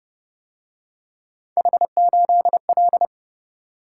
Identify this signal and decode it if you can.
Morse code, heard as a keyed tone, about 29 wpm, 700 Hz. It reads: H8L